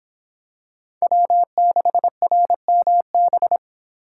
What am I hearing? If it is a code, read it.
Morse code, heard as a keyed tone, about 26 wpm, 700 Hz. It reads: W6RMB